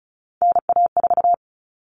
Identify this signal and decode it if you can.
Morse code, heard as a keyed tone, about 34 wpm, 700 Hz. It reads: NA4
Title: All Cut Numbers - 3-Digit 35wpm